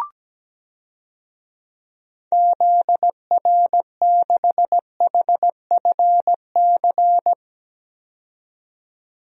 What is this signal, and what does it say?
Morse code, heard as a keyed tone, about 17 wpm, 700 Hz. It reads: ZR6HFC
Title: DXCC-100 Call Signs - 6 Characters - 17wpm